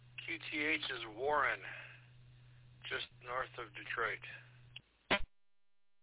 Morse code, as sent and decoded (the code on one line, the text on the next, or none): none